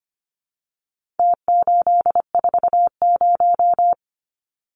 Morse code, heard as a keyed tone, about 25 wpm, 700 Hz. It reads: T840
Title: Numbers - Contest Style - 4-Digit 25wpm